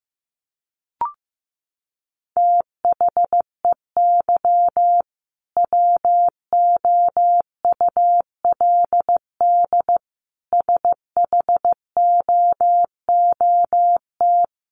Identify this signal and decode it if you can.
Morse code, heard as a keyed tone, about 15 wpm, 700 Hz. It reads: THEY WOULD SHOOT
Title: Sentences from Top 1200 Words 15wpm